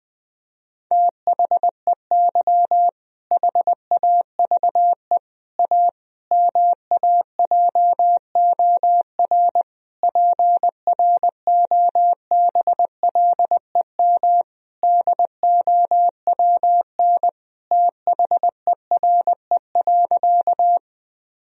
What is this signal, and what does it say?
Morse code, heard as a keyed tone, about 20 wpm, 700 Hz. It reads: THEY HAVE A MAJOR PROBLEM DOWN THERE.
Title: Sentences from Top 1000 Words - Review Part 4 - 20wpm